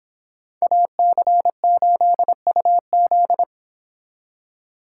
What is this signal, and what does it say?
Morse code, heard as a keyed tone, about 25 wpm, 700 Hz. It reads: AC8UZ